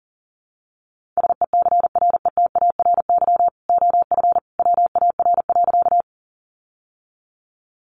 Morse code, heard as a keyed tone, about 37 wpm, 700 Hz. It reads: SECRETARY OF WAR.
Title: Sentences from Top 1300 Words 40wpm